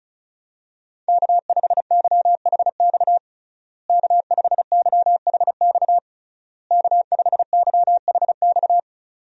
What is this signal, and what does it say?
Morse code, heard as a keyed tone, about 36 wpm, 700 Hz. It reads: K5YHX K5YHX K5YHX
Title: USA Call Signs - 5 Characters - Repeated 3x - 35wpm